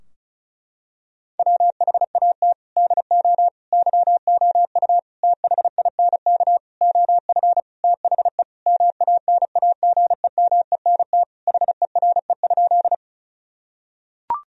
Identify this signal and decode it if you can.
Morse code, heard as a keyed tone, about 34 wpm, 700 Hz. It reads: WHAT DO YOU THINK OF THE MANAGEMENT HERE?
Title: Sentences from Top 1000 Words - Review Part 4 - 35wpm